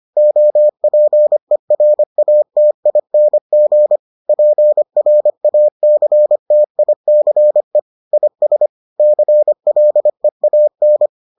Morse code, heard as a keyed tone, about 25 wpm, 600 Hz. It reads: OPERATING PRACTICE IS CLEAN